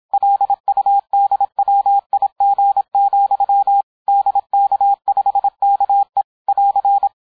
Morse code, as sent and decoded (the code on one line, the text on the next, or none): .-.. ..- -.. .-- .. --. --..-- / -.. -.- ..... -.- . / .-.-.
LUDWIG, DK5KE <AR>